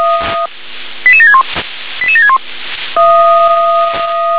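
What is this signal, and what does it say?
Five-tone selective call (ZVEI), heard as quick runs of five tones, tones of about 70 ms. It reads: 88761 88761
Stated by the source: Five-tone ZVEI selective call received by SDR.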